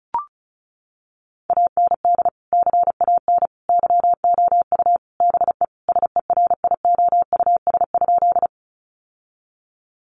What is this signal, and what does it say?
Morse code, heard as a keyed tone, about 34 wpm, 700 Hz. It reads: AND CAN YOU BE SERIOUS?
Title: Sentences from Top 1100 Words 35wpm